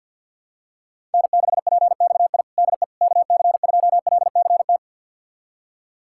Morse code, heard as a keyed tone, about 47 wpm, 700 Hz. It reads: N6PXI DE KC2LCT
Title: USA Callsign DE Callsign 50wpm